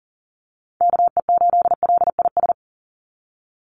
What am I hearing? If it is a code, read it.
Morse code, heard as a keyed tone, about 40 wpm, 700 Hz. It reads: KE8LIS
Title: USA Call Signs 40wpm